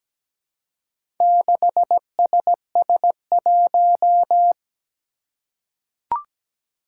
Morse code, heard as a keyed tone, about 17 wpm, 700 Hz. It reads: 6SS1
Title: Four Challenging Letter-Number Combinations 17wpm